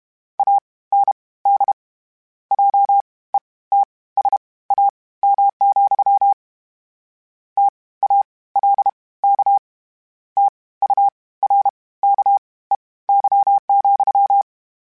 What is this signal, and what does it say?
Morse code, heard as a keyed tone, about 32 wpm, spaced as for 19 wpm, 800 Hz. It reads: AND JETSAM, TALK TURKEY,
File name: Misc / Misc-mp3s / WordPairs sp @30